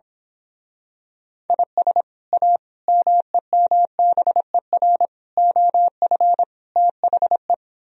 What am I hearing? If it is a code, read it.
Morse code, heard as a keyed tone, about 26 wpm, 700 Hz. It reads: IS A MEMBER OF THE
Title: Speed-Racing - Sets of 5 Words 22wpm